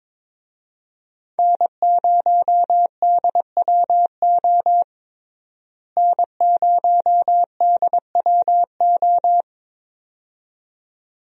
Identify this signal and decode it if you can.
Morse code, heard as a keyed tone, about 22 wpm, 700 Hz. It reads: N0DWO N0DWO
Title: USA Call Signs - 5 Characters - Repeated 2x - 22wpm